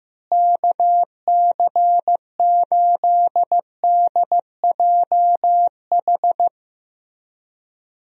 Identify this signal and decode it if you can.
Morse code, heard as a keyed tone, about 15 wpm, 700 Hz. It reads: KC8DJH
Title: USA Call Signs - 6 Characters - Repeated 3x - 15wpm